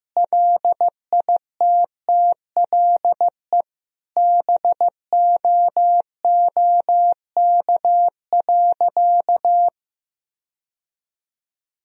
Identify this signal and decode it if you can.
Morse code, heard as a keyed tone, about 15 wpm, 700 Hz. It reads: LITTLE BOOK.
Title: Sentences from Top 1000 Words - Review Part 2 - 15wpm